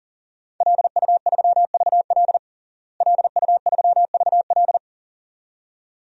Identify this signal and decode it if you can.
Morse code, heard as a keyed tone, about 40 wpm, 700 Hz. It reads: LU3VL LU3VL
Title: DXCC-100 Call Signs - 5 Characters - Repeated 2x - 40wpm